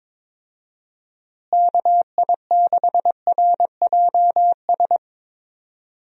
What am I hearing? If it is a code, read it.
Morse code, heard as a keyed tone, about 22 wpm, 700 Hz. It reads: KI6RJS
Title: USA Call Signs - 6 Characters - Repeated 2x - 22wpm